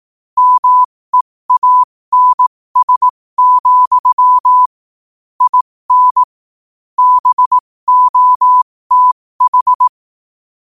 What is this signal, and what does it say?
Morse code, heard as a keyed tone, about 18 wpm, spaced as for 15 wpm, 1000 Hz. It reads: MEANS, IN BOTH
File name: qotdfiles / 15wpm